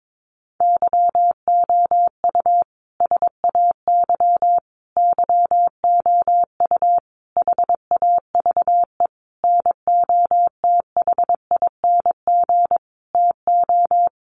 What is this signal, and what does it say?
Morse code, heard as a keyed tone, about 22 wpm, 700 Hz. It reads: YOU SAY YOU HAVE NOTHING TO
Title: Sentences from Top 700 Words 22wpm